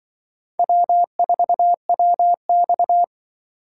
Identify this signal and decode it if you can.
Morse code, heard as a keyed tone, about 24 wpm, 700 Hz. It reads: W4WX
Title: Speed Racing - USA 1x2 Call Signs 20wpm